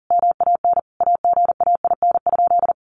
Morse code, heard as a keyed tone, about 40 wpm, 700 Hz. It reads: MAN AGAIN?
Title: Sentences from Top 1400 Words 40wpm